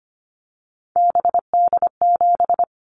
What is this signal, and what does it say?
Morse code, heard as a keyed tone, about 25 wpm, 700 Hz. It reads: BD7
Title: Callsign Prefixes for Top 100 DXCC - 25wpm